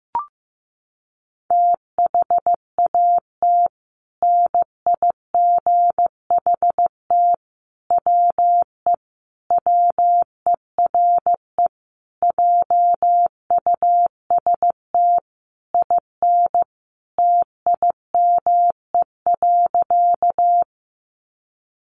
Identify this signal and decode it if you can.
Morse code, heard as a keyed tone, about 15 wpm, 700 Hz. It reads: THAT NIGHT WE WERE JUST IN TIME.
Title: Sentences from Top 300 Words 15wpm